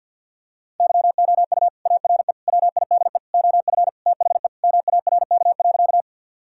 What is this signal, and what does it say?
Morse code, heard as a keyed tone, about 50 wpm, 700 Hz. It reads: YOU ARE WIDE OF THE MARK.
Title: Sentences from Top 1000 Words - Review Part 6 - 50wpm